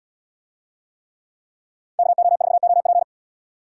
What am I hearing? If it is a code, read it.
Morse code, heard as a keyed tone, about 71 wpm, 700 Hz. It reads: KO4OP